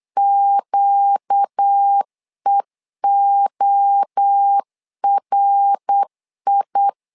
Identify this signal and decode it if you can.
Morse code, heard as a keyed tone, about 8 wpm, 800 Hz. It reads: QEORI
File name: LEZIONI / 4 Q1 8wpm 3035